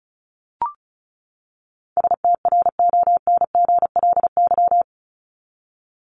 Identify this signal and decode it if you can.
Morse code, heard as a keyed tone, about 34 wpm, 700 Hz. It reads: STRONGLY
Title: LY Suffix - Top 100 Words with Base Words 35wpm